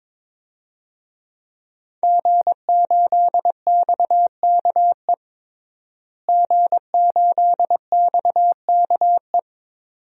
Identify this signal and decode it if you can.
Morse code, heard as a keyed tone, about 22 wpm, 700 Hz. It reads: G8XKE G8XKE
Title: DXCC-100 Call Signs - 5 Characters - Repeated 2x - 22wpm